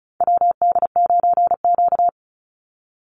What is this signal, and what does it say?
Morse code, heard as a keyed tone, about 34 wpm, 700 Hz. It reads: WD9Q
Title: Call Signs - 35wpm